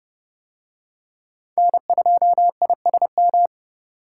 Morse code, heard as a keyed tone, about 27 wpm, 700 Hz. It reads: N2ISM